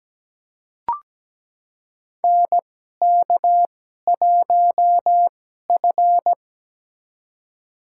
Morse code, heard as a keyed tone, about 17 wpm, spaced as for 12 wpm, 700 Hz. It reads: NK1F
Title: USA Call Signs - 4 Characters - 2x Character Spacing - 17wpm